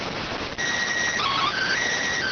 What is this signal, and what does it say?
SSTV (slow-scan television), PD 290.